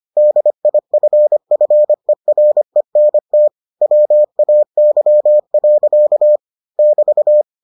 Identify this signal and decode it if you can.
Morse code, heard as a keyed tone, about 25 wpm, 600 Hz. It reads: DIFFERENT WAY. <BT>